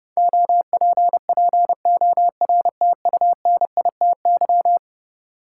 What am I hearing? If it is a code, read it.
Morse code, heard as a keyed tone, about 30 wpm, 700 Hz. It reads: OPPORTUNITY
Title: Sentences from Top 1000 Words - New Word Drill - 30wpm